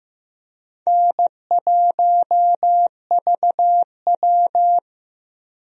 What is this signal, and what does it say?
Morse code, heard as a keyed tone, about 15 wpm, 700 Hz. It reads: N1VW
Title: USA Call Signs - 4 Characters - Repeated 2x - 15wpm